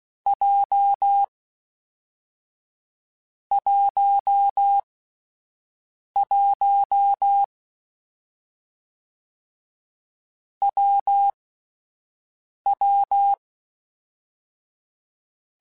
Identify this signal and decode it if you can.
Morse code, heard as a keyed tone, about 16 wpm, 800 Hz. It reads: J 1 1 W W
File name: Difficult Combos / Difficult Combos-mp3s / WPJ WPJ1 @10